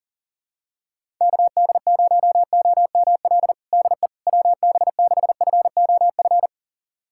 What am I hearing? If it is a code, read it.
Morse code, heard as a keyed tone, about 40 wpm, 700 Hz. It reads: KD0OML DE WB6FOF
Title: USA Callsign DE Callsign 40wpm